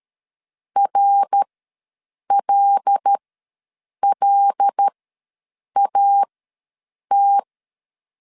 Morse code, heard as a keyed tone, about 13 wpm, spaced as for 7 wpm, 800 Hz. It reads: RLLAT
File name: LEZIONI / 5 LBVC / 5 L ws 12wpm 2958